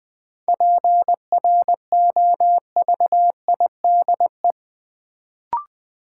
Morse code, heard as a keyed tone, about 20 wpm, 700 Hz. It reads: PROVIDE
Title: Sentences from Top 1000 Words - Review Part 2 - New Word Drill - 20wpm